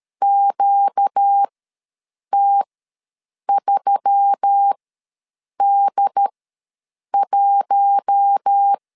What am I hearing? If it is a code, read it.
Morse code, heard as a keyed tone, about 13 wpm, spaced as for 7 wpm, 800 Hz. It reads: QT3D1
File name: LEZIONI / 6 PKX0 / 6 P ws 12wpm 3117